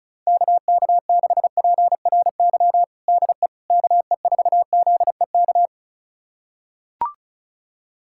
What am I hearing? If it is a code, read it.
Morse code, heard as a keyed tone, about 36 wpm, 700 Hz. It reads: KK6PRY DE KE4ZEK
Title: USA Callsign DE Callsign - 6 Characters - 35wpm